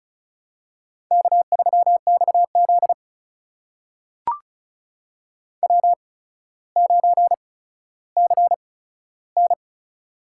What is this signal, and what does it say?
Morse code, heard as a keyed tone, about 36 wpm, spaced as for 15 wpm, 700 Hz. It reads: K3XZ W9CN
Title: USA Call Signs - 4 Characters - 8x Character Spacing - 35wpm